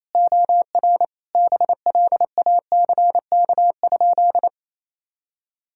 Morse code, heard as a keyed tone, about 28 wpm, 700 Hz. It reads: OR BLACK?
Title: Sentences from Top 400 Words and Abbreviations 28wpm